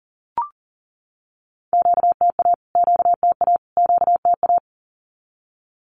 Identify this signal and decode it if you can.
Morse code, heard as a keyed tone, about 39 wpm, 700 Hz. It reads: QTA QTA QTA
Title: Q-Codes Repeated 3x 40wpm